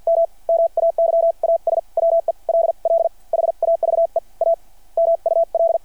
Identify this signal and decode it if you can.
Morse code, heard as a keyed tone, about 52 wpm, spaced as for 42 wpm, 660 Hz. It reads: M MAYAS WELL HAVE A MUL
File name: archived / CRSnet